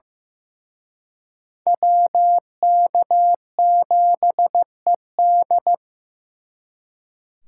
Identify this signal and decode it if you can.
Morse code, heard as a keyed tone, about 18 wpm, 700 Hz. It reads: WK7ED
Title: Speed Racing - USA 2x2 Call Signs 15wpm